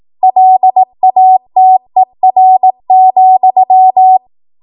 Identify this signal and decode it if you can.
Morse code, heard as a keyed tone, about 18 wpm, 750 Hz. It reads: LATER,
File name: Archive / 18 WPM / 18WPM